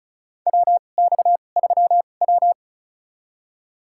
Morse code, heard as a keyed tone, about 34 wpm, spaced as for 25 wpm, 700 Hz. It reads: WX3W
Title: USA Call Signs - 4 Characters - 2x Character Spacing - 35wpm